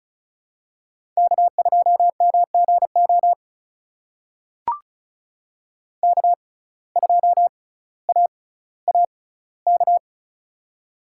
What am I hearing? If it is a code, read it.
Morse code, heard as a keyed tone, about 36 wpm, spaced as for 18 wpm, 700 Hz. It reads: K2MGO K2AAK
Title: USA Call Signs - 5 Characters - 6x Character Spacing - 35wpm